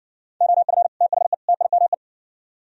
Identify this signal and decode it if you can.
Morse code, heard as a keyed tone, about 59 wpm, 700 Hz. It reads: OF THE TIME